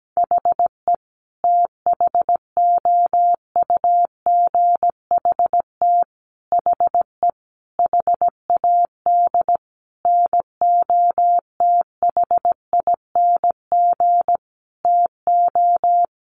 Morse code, heard as a keyed tone, about 17 wpm, 700 Hz. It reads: HE THOUGHT HE HAD NOTHING TO
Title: Sentences from Top 600 Words 17wpm